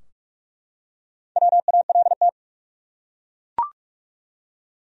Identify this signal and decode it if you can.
Morse code, heard as a keyed tone, about 41 wpm, 700 Hz. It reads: WALT
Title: USA Common Names - 45wpm